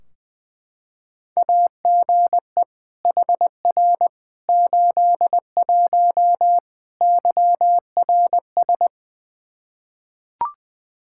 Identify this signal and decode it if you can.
Morse code, heard as a keyed tone, about 20 wpm, 700 Hz. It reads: AGE HR 81 YRS
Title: Practice QSO - Exchange 3 - More Info - 20wpm